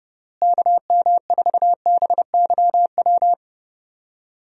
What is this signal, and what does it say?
Morse code, heard as a keyed tone, about 30 wpm, 700 Hz. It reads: KM4BYW
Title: USA Call Signs - 6 Characters - Repeated 3x - 30wpm